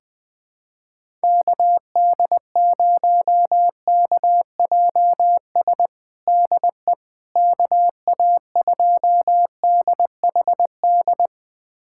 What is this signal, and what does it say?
Morse code, heard as a keyed tone, about 20 wpm, 700 Hz. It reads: KD0KJS DE KA2DHD
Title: USA Callsign DE Callsign - 6 Characters - 20wpm